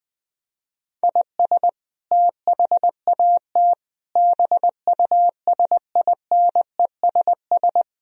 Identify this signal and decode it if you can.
Morse code, heard as a keyed tone, about 20 wpm, 700 Hz. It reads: IS THAT BUSINESS
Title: Sentences from Top 1000 Words - Review Part 3 - 20wpm